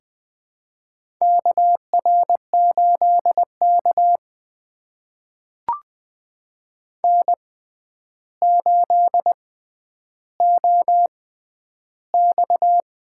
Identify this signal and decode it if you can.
Morse code, heard as a keyed tone, about 21 wpm, spaced as for 10 wpm, 700 Hz. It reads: KR8K N8OX
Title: USA Call Signs - 6x Character Spacing - 20wpm